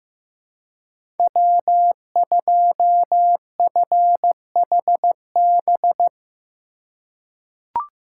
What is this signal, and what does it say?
Morse code, heard as a keyed tone, about 15 wpm, 700 Hz. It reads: W2FHB
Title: USA Call Signs - 3x Character Spacing - 15wpm